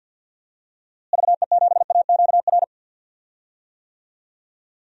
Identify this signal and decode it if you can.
Morse code, heard as a keyed tone, about 53 wpm, 700 Hz. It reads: VE7AQR
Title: DXCC-100 Call Signs - 6 Characters - Repeated 2x - 50wpm